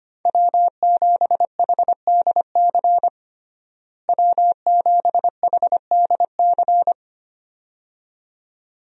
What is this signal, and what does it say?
Morse code, heard as a keyed tone, about 25 wpm, 700 Hz. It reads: W7HDC W7HDC
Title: USA Call Signs - 5 Characters - Repeated 2x - 25wpm